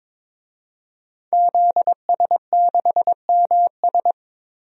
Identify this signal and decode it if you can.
Morse code, heard as a keyed tone, about 21 wpm, 700 Hz. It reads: ZS6MS
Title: DXCC-100 Call Signs - 5 Characters - 22wpm